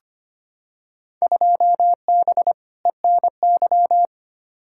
Morse code, heard as a keyed tone, about 25 wpm, 700 Hz. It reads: 2B ENY
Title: Field Day Exchanges Encore - 25wpm